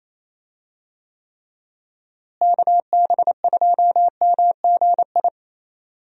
Morse code, heard as a keyed tone, about 27 wpm, 700 Hz. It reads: KB2MGI
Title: USA Call Signs - 6 Characters - Repeated 2x - 28wpm